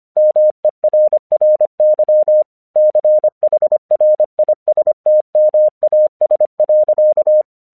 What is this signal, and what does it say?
Morse code, heard as a keyed tone, about 25 wpm, 610 Hz. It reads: MERRY CHRISTMAS.
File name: 1 English 25wpm / AQC34